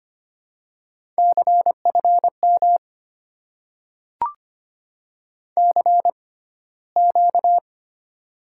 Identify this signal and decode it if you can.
Morse code, heard as a keyed tone, about 25 wpm, spaced as for 15 wpm, 700 Hz. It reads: CFM CQ
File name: Top 90 QSO Elements - 6x Character Spacing - 25wpm